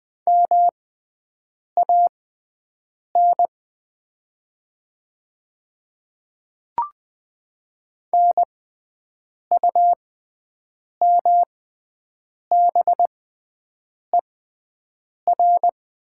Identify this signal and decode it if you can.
Morse code, heard as a keyed tone, about 20 wpm, spaced as for 7 wpm, 700 Hz. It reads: MAN NUMBER
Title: 6x Character Spacing - Top 300 Words - 20wpm